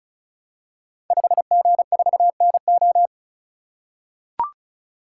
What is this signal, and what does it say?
Morse code, heard as a keyed tone, about 36 wpm, 700 Hz. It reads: 5G4NO
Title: Code Groups - 5 Letter:Numbers - 35wpm